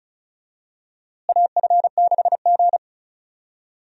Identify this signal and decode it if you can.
Morse code, heard as a keyed tone, about 36 wpm, 700 Hz. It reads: AF6G
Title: USA Call Signs - 4 Characters - Repeated 3x - 35wpm